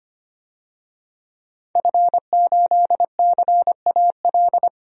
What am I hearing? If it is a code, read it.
Morse code, heard as a keyed tone, about 25 wpm, 700 Hz. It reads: F8CAL
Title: DXCC-100 Call Signs - 5 Characters - 25wpm